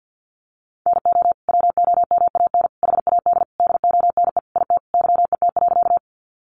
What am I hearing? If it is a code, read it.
Morse code, heard as a keyed tone, about 48 wpm, 700 Hz. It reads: NO WOMAN HAD DONE IT YET.
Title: Sentences from Top 1100 Words 50wpm